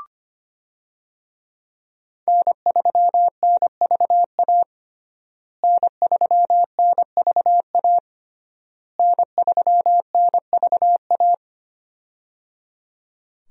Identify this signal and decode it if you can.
Morse code, heard as a keyed tone, about 25 wpm, 700 Hz. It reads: N3NVA N3NVA N3NVA
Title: USA Call Signs - 5 Characters - Repeated 3x - 25wpm